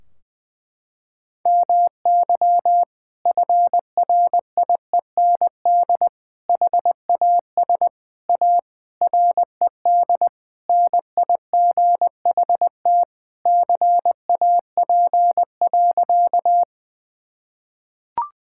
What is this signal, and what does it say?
Morse code, heard as a keyed tone, about 20 wpm, 700 Hz. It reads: MY FRIEND HAS A RED NIGHT CAP.
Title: Sentences from Top 1600 Words 20wpm